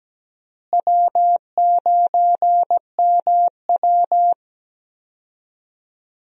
Morse code, heard as a keyed tone, about 17 wpm, 700 Hz. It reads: W9MW
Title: USA Call Signs - 4 Characters - Repeated 2x - 17wpm